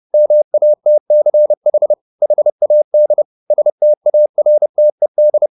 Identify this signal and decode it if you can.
Morse code, heard as a keyed tone, about 30 wpm, 600 Hz. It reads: MATCH HAD STARTED